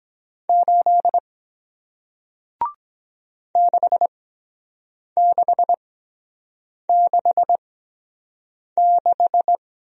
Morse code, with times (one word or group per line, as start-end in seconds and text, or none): 0.5-1.2 8
3.6-9.6 6666